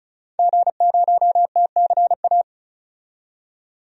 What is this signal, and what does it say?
Morse code, heard as a keyed tone, about 36 wpm, 700 Hz. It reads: G0TCA
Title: DXCC-100 Call Signs - 5 Characters - 35wpm